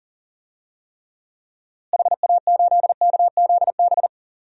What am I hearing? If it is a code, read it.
Morse code, heard as a keyed tone, about 40 wpm, 700 Hz. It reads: HA8KZB
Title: DXCC-100 Call Signs - 6 Characters - Repeated 3x - 40wpm